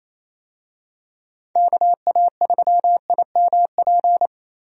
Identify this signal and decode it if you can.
Morse code, heard as a keyed tone, about 27 wpm, 700 Hz. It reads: KA3IMP
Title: USA Call Signs - 6 Characters - 28wpm